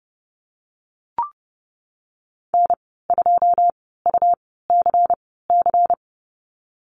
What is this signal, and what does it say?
Morse code, heard as a keyed tone, about 30 wpm, spaced as for 17 wpm, 700 Hz. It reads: N2UCC
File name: Lesson 19g - Letter w - Call Signs 30wpm